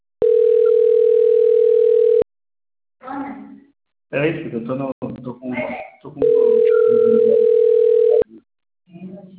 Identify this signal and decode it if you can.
Ringback tone (North America), 440 + 480 Hz, with speech.
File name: rg-1600-unknown- - - .79899